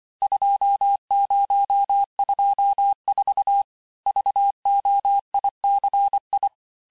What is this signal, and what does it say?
Morse code, heard as a keyed tone, about 25 wpm, 800 Hz. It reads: 2024 VOICI
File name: texte-chiffre-vitesse-23mpm